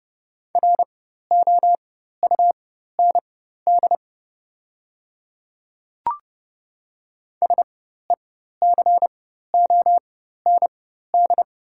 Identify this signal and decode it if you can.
Morse code, heard as a keyed tone, about 30 wpm, spaced as for 14 wpm, 700 Hz. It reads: ROUND SECOND
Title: 4x Character Spacing - Top 300 Words - 30wpm